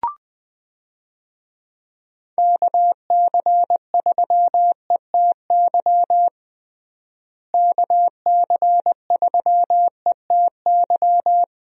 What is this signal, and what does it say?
Morse code, heard as a keyed tone, about 21 wpm, 700 Hz. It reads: KC3ETY KC3ETY
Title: USA Call Signs - 6 Characters - Repeated 2x - 20wpm